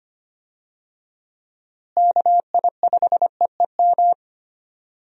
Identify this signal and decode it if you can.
Morse code, heard as a keyed tone, about 25 wpm, 700 Hz. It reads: KI5EEM
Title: USA Call Signs - 6 Characters - Repeated 2x - 25wpm